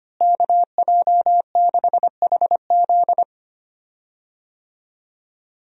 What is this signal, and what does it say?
Morse code, heard as a keyed tone, about 25 wpm, 700 Hz. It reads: KJ6HZ
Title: USA Call Signs - 5 Characters - Repeated 2x - 25wpm